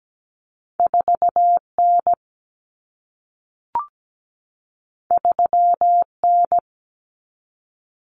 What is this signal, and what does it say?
Morse code, heard as a keyed tone, about 17 wpm, 700 Hz. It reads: 4N 3N
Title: Common Cut Numbers - 2-Digit 17wpm